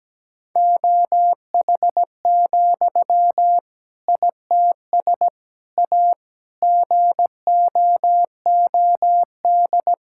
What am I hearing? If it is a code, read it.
Morse code, heard as a keyed tone, about 17 wpm, 700 Hz. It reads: OH, ITS A GOOD
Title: Sentences from Top 1000 Words - Review Part 5 - 17wpm